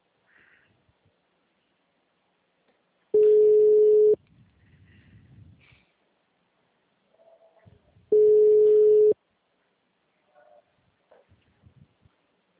Ringback tone (Europe, 425 Hz).
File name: out- -2016- - - .48780